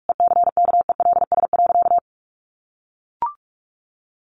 Morse code, heard as a keyed tone, about 44 wpm, 700 Hz. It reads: ECKELS.
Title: A Sound Of Thunder 45wpm